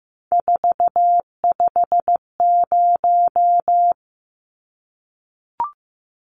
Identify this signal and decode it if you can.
Morse code, heard as a keyed tone, about 15 wpm, 700 Hz. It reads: 450
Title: Numbers - 3-Digit 15wpm